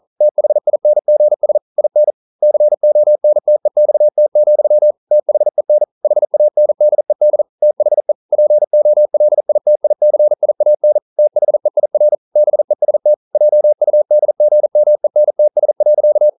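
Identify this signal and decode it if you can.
Morse code, heard as a keyed tone, about 41 wpm, 600 Hz. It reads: THINGS IN CONTEXT, THEN HANDED THE POLITICIAN THEIR BEST JUDGMENTS.
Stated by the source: Morse code practice at 41 wpm, from CW med Gnister.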